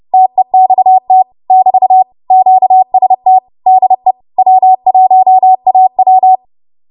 Morse code, heard as a keyed tone, about 30 wpm, 750 Hz. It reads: TEXT <BT> QST DE W1AW
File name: Morse / 30WPM